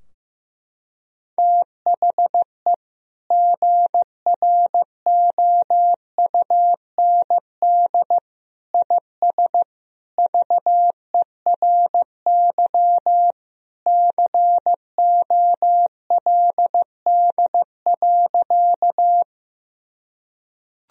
Morse code, heard as a keyed tone, about 15 wpm, 700 Hz. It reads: THE GROUND IS VERY COLD.
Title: Sentences from Top 1000 Words - Review Part 2 - 15wpm